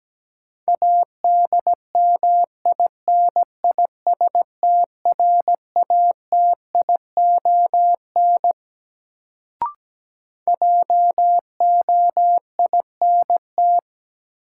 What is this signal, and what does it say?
Morse code, heard as a keyed tone, about 17 wpm, 700 Hz. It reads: ADMINISTRATION JOINT
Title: Sentences from Top 1300 Words - New Word Drill Alt - 17wpm